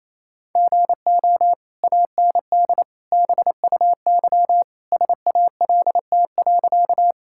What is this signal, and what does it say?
Morse code, heard as a keyed tone, about 28 wpm, 700 Hz. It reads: GO AND BUY SALT.
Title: Sentences from Top 1000 Words - Review Part 5 - 28wpm